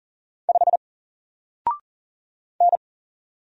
Morse code, heard as a keyed tone, about 39 wpm, 700 Hz. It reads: 5 N
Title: Single Character (Letter:Number:Punctuation:Prosign) 40wpm